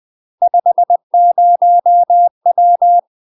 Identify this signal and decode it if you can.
Morse code, heard as a keyed tone, about 20 wpm, 700 Hz. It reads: 50W
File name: UK MASTER QSOs TRAINING / 20WPM / UK MASTER QSOs 0008